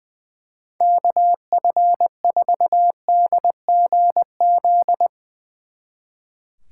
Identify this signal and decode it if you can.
Morse code, heard as a keyed tone, about 21 wpm, 700 Hz. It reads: KF4DGZ
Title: Speed Racing - USA 2x3 Call Signs 20wpm